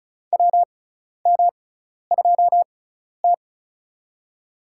Morse code, heard as a keyed tone, about 34 wpm, spaced as for 12 wpm, 700 Hz. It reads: WM2T
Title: USA Call Signs - 4 Characters - 6x Character Spacing - 35wpm